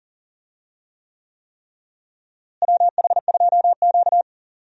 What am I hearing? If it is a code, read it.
Morse code, heard as a keyed tone, about 38 wpm, 700 Hz. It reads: WH2Q
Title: USA Call Signs - 4 Characters - Repeated 3x - 40wpm